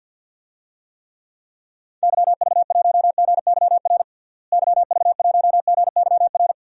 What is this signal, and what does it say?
Morse code, heard as a keyed tone, about 52 wpm, 700 Hz. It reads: YV1GYR YV1GYR
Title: DXCC-100 Call Signs - 6 Characters - Repeated 2x - 50wpm